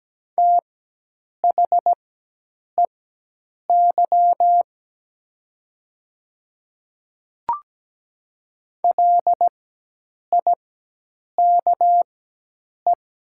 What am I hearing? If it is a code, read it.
Morse code, heard as a keyed tone, about 17 wpm, spaced as for 8 wpm, 700 Hz. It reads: THEY LIKE
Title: 4x Character Spacing - Top 200 Words - 17wpm